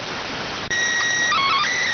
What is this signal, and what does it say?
SSTV (slow-scan television), PD 290.